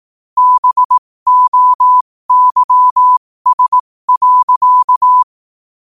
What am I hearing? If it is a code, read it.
Morse code, heard as a keyed tone, about 18 wpm, spaced as for 15 wpm, 1000 Hz. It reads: BOYS.
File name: qotdfiles / 15wpm